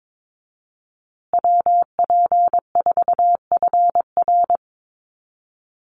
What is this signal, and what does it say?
Morse code, heard as a keyed tone, about 22 wpm, 700 Hz. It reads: WP4FR
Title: USA Call Signs Repeated 2x 22wpm